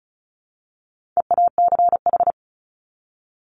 Morse code, heard as a keyed tone, about 34 wpm, 700 Hz. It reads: EACH
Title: Top 200 Words 35wpm